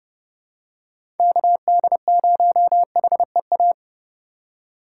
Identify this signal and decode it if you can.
Morse code, heard as a keyed tone, about 30 wpm, 700 Hz. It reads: KD0HEA
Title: USA Call Signs - 6 Characters - 4x Character Spacing - 30wpm